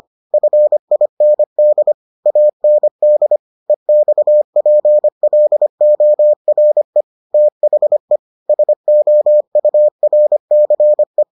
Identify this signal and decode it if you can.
Morse code, heard as a keyed tone, about 25 wpm, 600 Hz. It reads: FIND AND EXPLORE THE SOURCE